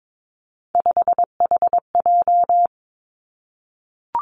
Morse code, heard as a keyed tone, about 22 wpm, 700 Hz. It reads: 5HJ
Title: Three Challenging Letter-Number Combinations 22wpm